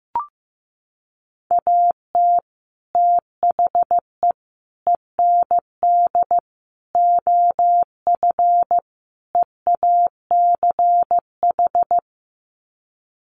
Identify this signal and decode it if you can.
Morse code, heard as a keyed tone, about 15 wpm, 700 Hz. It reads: AT THE END OF EACH
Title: Sets of 5 Words Encore 15wpm